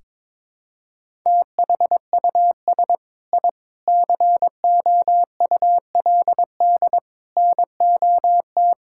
Morse code, heard as a keyed tone, about 21 wpm, 700 Hz. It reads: THUS I COULD NOT
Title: Sentences from Top 1000 Words - Review Part 5 - 22wpm